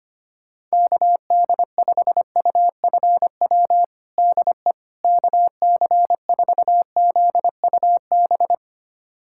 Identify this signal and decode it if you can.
Morse code, heard as a keyed tone, about 25 wpm, 700 Hz. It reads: KD5UFW DE KC4ZUB
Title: USA Callsign DE Callsign - 6 Characters - 25wpm